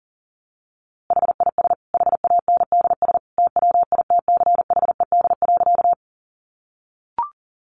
Morse code, heard as a keyed tone, about 37 wpm, 700 Hz. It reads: HIS HANDS TWITCHED.
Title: A Sound Of Thunder 40wpm